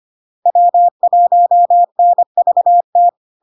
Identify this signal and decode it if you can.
Morse code, heard as a keyed tone, about 25 wpm, 700 Hz. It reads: W1NVT
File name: LARK-FD-2022-Log-in-CW-25wpm-700-Hz